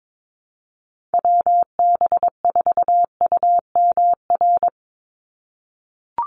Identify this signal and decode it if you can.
Morse code, heard as a keyed tone, about 22 wpm, 700 Hz. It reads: WB4UMR
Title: USA Call Signs 22wpm